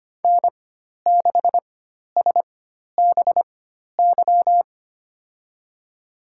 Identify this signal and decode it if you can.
Morse code, heard as a keyed tone, about 25 wpm, spaced as for 12 wpm, 700 Hz. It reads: N6SBY
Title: USA Call Signs - 5 Characters - 4x Character Spacing - 25wpm